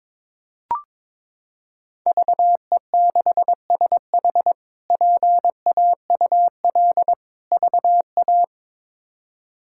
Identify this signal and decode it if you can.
Morse code, heard as a keyed tone, about 22 wpm, 700 Hz. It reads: VE6SH PAUL VA
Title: K1USN SST QSO Practice - 22wpm